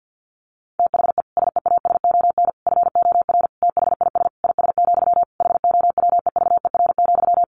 Morse code, heard as a keyed tone, about 50 wpm, 700 Hz. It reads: THE SEASON FOR THIS IS, HOWEVER,
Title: Sentences from Top 1400 Words 50wpm